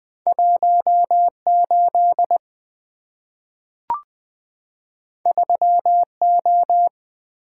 Morse code, heard as a keyed tone, about 20 wpm, 700 Hz. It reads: 18 3O
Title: Two Letter-Number Combinations - 20wpm